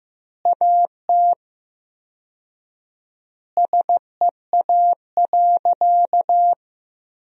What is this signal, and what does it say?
Morse code, heard as a keyed tone, about 15 wpm, 700 Hz. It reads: AT SEA.
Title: 4x Word-Spacing - Sentences from Top 300 Words - 15wpm